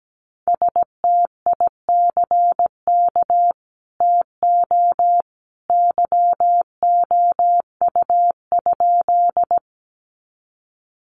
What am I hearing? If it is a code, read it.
Morse code, heard as a keyed tone, about 17 wpm, 700 Hz. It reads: STICK TO YOU?
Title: Sentences from Top 1200 Words 17wpm